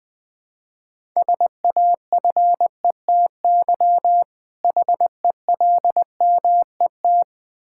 Morse code, heard as a keyed tone, about 21 wpm, 700 Hz. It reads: SAFETY HELMET
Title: Difficult Words with Hints - 20wpm